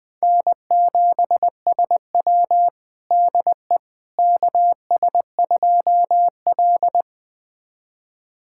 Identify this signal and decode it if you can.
Morse code, heard as a keyed tone, about 21 wpm, 700 Hz. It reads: N7SW DE KS2L
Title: USA Callsign DE Callsign - 4 Characters - 20wpm